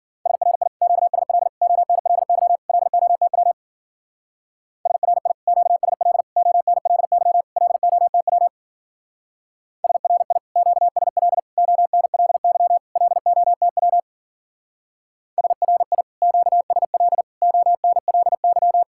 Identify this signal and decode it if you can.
Morse code, heard as a keyed tone, about 52 wpm, 700 Hz. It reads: SRI QSL ONLY LOTW SRI QSL ONLY LOTW SRI QSL ONLY LOTW SRI QSL ONLY